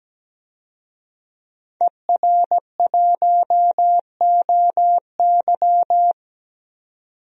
Morse code, heard as a keyed tone, about 17 wpm, 700 Hz. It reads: ER1OY
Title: DXCC-100 Call Signs - 5 Characters - 17wpm